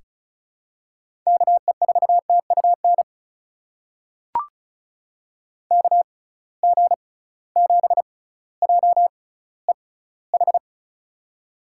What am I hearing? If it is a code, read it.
Morse code, heard as a keyed tone, about 36 wpm, spaced as for 18 wpm, 700 Hz. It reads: KE4TUN KG7JEH
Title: USA Call Signs - 6 Characters - 6x Character Spacing - 35wpm